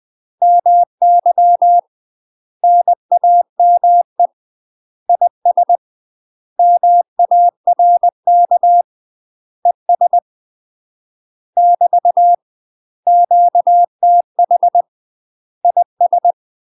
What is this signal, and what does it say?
Morse code, heard as a keyed tone, about 20 wpm, 700 Hz. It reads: MY NAME IS MARK ES <BT> QTH IS
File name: UK MASTER QSOs TRAINING / 20WPM / UK MASTER QSOs 0005